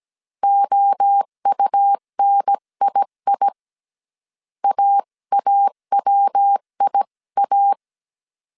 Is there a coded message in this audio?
Yes: Morse code from 0.4 to 7.7 s, heard as a keyed tone, about 17 wpm, 800 Hz. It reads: OUNII AAWIA